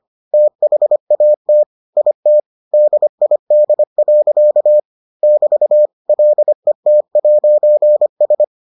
Morse code, heard as a keyed tone, about 25 wpm, 600 Hz. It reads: THAT IT DID. <BT> LET'S